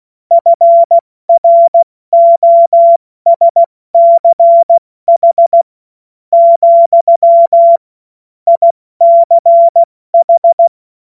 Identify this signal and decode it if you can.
Morse code, heard as a keyed tone, about 16 wpm, spaced as for 13 wpm, 680 Hz. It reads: FROSCH, ICH
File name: Morsekurs / Witze / Witz 08. November 2022 70 F80